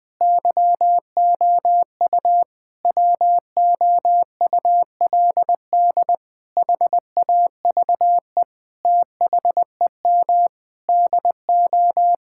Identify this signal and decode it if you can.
Morse code, heard as a keyed tone, about 20 wpm, 700 Hz. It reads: YOU WOULD HAVE THEM DO
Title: Sentences with 9-Word Phrases - Spoken Beginning - 20wpm